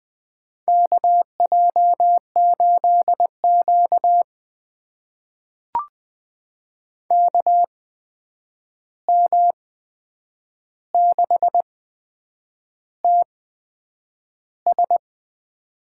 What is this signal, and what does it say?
Morse code, heard as a keyed tone, about 20 wpm, spaced as for 8 wpm, 700 Hz. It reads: KJ8Q KM6TS
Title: USA Call Signs - 8x Character Spacing - 20wpm